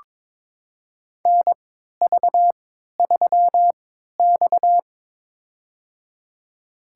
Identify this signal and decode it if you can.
Morse code, heard as a keyed tone, about 22 wpm, spaced as for 13 wpm, 700 Hz. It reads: NV3X
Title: USA Call Signs - 4 Characters - 3x Character Spacing - 22wpm